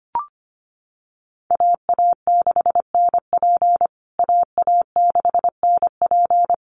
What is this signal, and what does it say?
Morse code, heard as a keyed tone, about 25 wpm, 700 Hz. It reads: AA6NP AA6NP
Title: USA Call Signs Repeated 3x 25wpm